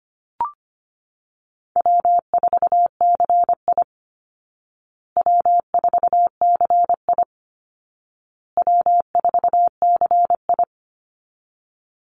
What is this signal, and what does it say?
Morse code, heard as a keyed tone, about 25 wpm, 700 Hz. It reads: W4CI W4CI W4CI
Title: Call Signs Repeated 2x - 25wpm